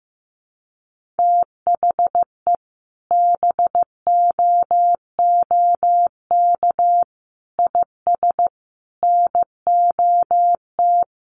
Morse code, heard as a keyed tone, about 15 wpm, 700 Hz. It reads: THE BOOK IS NOT
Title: Sentences from Top 1400 Words 15wpm